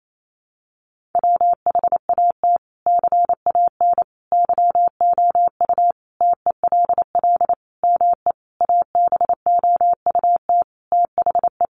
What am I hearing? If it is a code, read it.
Morse code, heard as a keyed tone, about 28 wpm, 700 Hz. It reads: WHAT CAN YOU TELL ME ABOUT THE
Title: Sentences from Top 1500 Words 28wpm